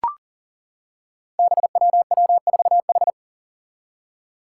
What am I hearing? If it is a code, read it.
Morse code, heard as a keyed tone, about 41 wpm, 700 Hz. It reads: BWW4H
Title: Five Challenging Letter-Number Combinations 40wpm